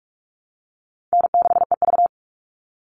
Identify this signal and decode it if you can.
Morse code, heard as a keyed tone, about 44 wpm, 700 Hz. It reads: N6EV
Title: Easy Call Signs Repeated 2x - 45wpm